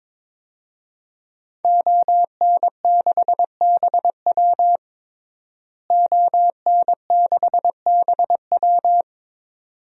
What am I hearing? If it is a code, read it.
Morse code, heard as a keyed tone, about 22 wpm, 700 Hz. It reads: ON6BW ON6BW
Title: DXCC-100 Call Signs - 5 Characters - Repeated 2x - 22wpm